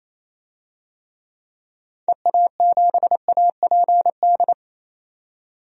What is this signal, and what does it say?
Morse code, heard as a keyed tone, about 28 wpm, 700 Hz. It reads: EA7APD
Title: DXCC-100 Call Signs - 6 Characters - 28wpm